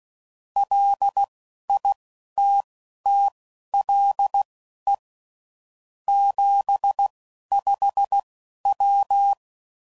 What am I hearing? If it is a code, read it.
Morse code, heard as a keyed tone, about 16 wpm, spaced as for 11 wpm, 800 Hz. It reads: LITTLE 75W
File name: CW Code Groups @10WPM / QSOs / QSO WA7QOX